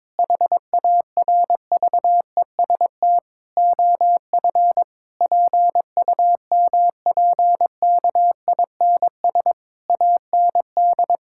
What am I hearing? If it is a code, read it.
Morse code, heard as a keyed tone, about 22 wpm, 700 Hz. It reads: HARVEST OF PUMPKINS AND
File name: Difficult Sentences with Hints - 22wpm